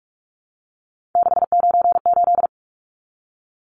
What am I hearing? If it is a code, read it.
Morse code, heard as a keyed tone, about 44 wpm, 700 Hz. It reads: 698
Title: Numbers - 3-Digit 45wpm